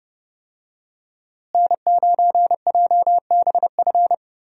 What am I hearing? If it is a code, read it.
Morse code, heard as a keyed tone, about 30 wpm, 700 Hz. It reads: N9JBF
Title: USA Call Signs - 5 Characters - Repeated 2x - 30wpm